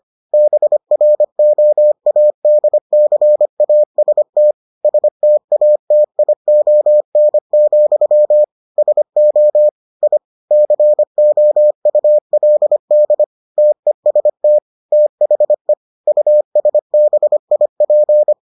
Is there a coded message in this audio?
Yes: Morse code from 0.3 to 18.3 s, heard as a keyed tone, about 25 wpm, 600 Hz. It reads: BROADCAST STATION, SO I COULD TEST THE USBIP